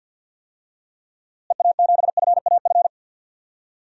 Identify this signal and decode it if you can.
Morse code, heard as a keyed tone, about 48 wpm, 700 Hz. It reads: EA7FAF
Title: DXCC-100 Call Signs - 6 Characters - Repeated 3x - 50wpm